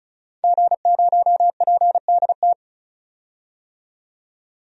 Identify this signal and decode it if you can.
Morse code, heard as a keyed tone, about 36 wpm, 700 Hz. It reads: G0PDT
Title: DXCC-100 Call Signs - 5 Characters - 35wpm